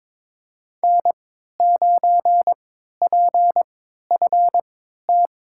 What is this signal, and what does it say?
Morse code, heard as a keyed tone, about 23 wpm, spaced as for 13 wpm, 700 Hz. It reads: N9PFT
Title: USA Call Signs - 5 Characters - 3x Character Spacing - 22wpm